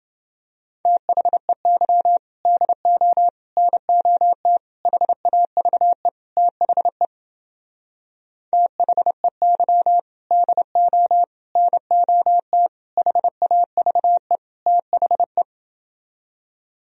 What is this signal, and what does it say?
Morse code, heard as a keyed tone, about 31 wpm, 700 Hz. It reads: THEY DO NOT HAVE THE THEY DO NOT HAVE THE